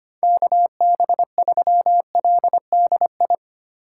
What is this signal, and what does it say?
Morse code, heard as a keyed tone, about 25 wpm, 700 Hz. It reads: KB3LDI
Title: USA Call Signs - 8x Character Spacing - 25wpm